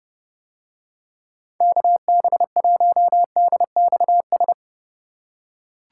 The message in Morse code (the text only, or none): KB1DXS